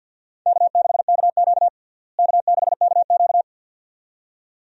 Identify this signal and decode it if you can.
Morse code, heard as a keyed tone, about 48 wpm, 700 Hz. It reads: K6KQ K6KQ
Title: USA Call Signs - 4 Characters - Repeated 2x - 50wpm